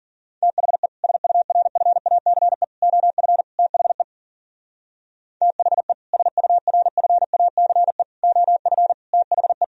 Morse code, heard as a keyed tone, about 42 wpm, 700 Hz. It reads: THE SURFACE OF THE THE SURFACE OF THE
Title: Speed-Racing - Sets of 4 Words 40wpm